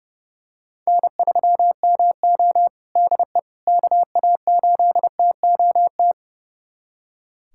Morse code, heard as a keyed tone, about 30 wpm, 700 Hz. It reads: N3MO DE KA8TOT
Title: USA Callsign DE Callsign 30wpm